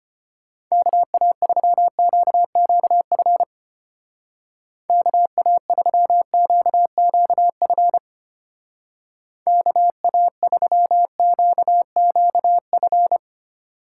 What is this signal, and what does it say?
Morse code, heard as a keyed tone, about 32 wpm, 700 Hz. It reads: KA3QQF KA3QQF KA3QQF